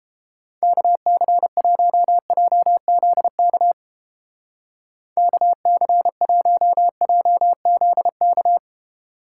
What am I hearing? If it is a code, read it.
Morse code, heard as a keyed tone, about 29 wpm, 700 Hz. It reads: KC1JZK KC1JZK